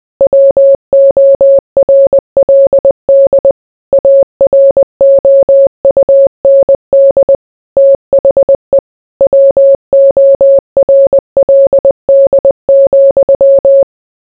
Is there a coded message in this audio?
Yes: Morse code from 0.2 to 13.8 s, heard as a keyed tone, about 20 wpm, 550 Hz. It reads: WORLD AROUND THE WORLD,